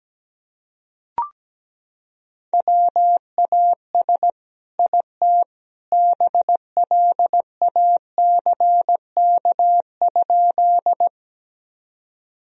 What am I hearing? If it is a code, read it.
Morse code, heard as a keyed tone, about 17 wpm, 700 Hz. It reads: WAS IT BLACK?
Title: Sentences from Top 1000 Words - Review Part 2 - 17wpm